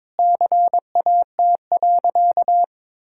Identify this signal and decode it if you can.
Morse code, heard as a keyed tone, about 21 wpm, 700 Hz. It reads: CAT.
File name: practice / 22wpm / Sentences with 3-Word Phrases - 22wpm